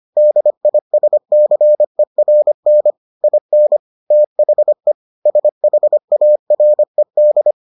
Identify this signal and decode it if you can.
Morse code, heard as a keyed tone, about 25 wpm, 600 Hz. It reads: DISCERN IN THE SHARED